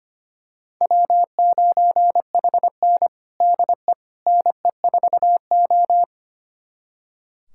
Morse code, heard as a keyed tone, about 25 wpm, 700 Hz. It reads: W9HN DE NE4O
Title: USA Callsign DE Callsign - 4 Characters - 25wpm